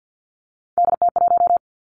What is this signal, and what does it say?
Morse code, heard as a keyed tone, about 50 wpm, 700 Hz. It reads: DT1
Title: Three Challenging Letter-Number Combinations 50wpm